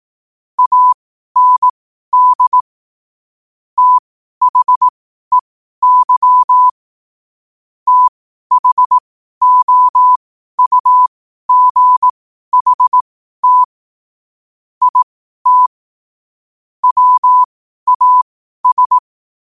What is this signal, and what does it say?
Morse code, heard as a keyed tone, about 18 wpm, spaced as for 12 wpm, 1000 Hz. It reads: AND THEY THOUGHT IT WAS
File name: qotdfiles / 12wpm